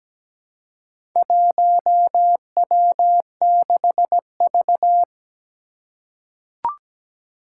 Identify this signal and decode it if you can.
Morse code, heard as a keyed tone, about 17 wpm, 700 Hz. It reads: 1W6V
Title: Four Challenging Letter-Number Combinations 17wpm